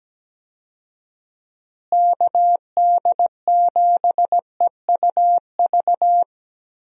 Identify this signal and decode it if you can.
Morse code, heard as a keyed tone, about 17 wpm, 700 Hz. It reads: KD7EUV
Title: USA Call Signs - 6 Characters - Repeated 3x - 17wpm